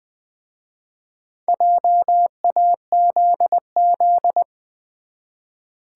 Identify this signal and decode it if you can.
Morse code, heard as a keyed tone, about 20 wpm, 700 Hz. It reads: JAZZ
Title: Easy Words with Hints - 20wpm